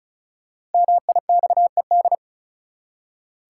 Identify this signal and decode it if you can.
Morse code, heard as a keyed tone, about 36 wpm, 700 Hz. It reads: MIXED
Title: Five Letter Words - 35wpm